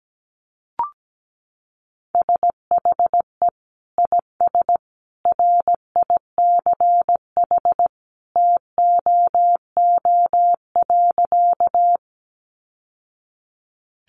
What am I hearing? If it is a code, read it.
Morse code, heard as a keyed tone, about 17 wpm, 700 Hz. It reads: SHE IS RICH TOO.
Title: Sentences from Top 1100 Words 17wpm